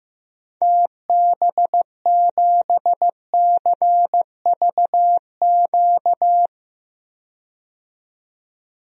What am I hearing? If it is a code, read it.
Morse code, heard as a keyed tone, about 15 wpm, 700 Hz. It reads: TB7CVQ
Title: DXCC-100 Call Signs - 6 Characters - 15wpm